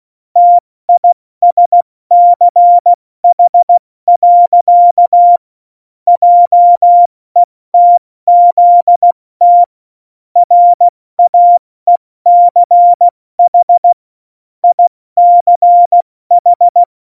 Morse code, heard as a keyed tone, about 16 wpm, 700 Hz. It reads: TISCH. JETZT RAECH ICH
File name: Morsekurs / Witze / Witz 15. Oktober 2019, 70 F80